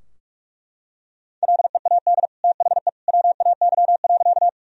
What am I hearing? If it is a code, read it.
Morse code, heard as a keyed tone, about 41 wpm, 700 Hz. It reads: LEAD THE WAY.
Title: Sentences from Top 1000 Words - Review Part 2 - 45wpm